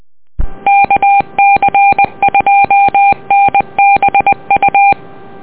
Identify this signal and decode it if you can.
Morse code, heard as a keyed tone, about 20 wpm, 800 Hz. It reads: KC2NBU
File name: net audio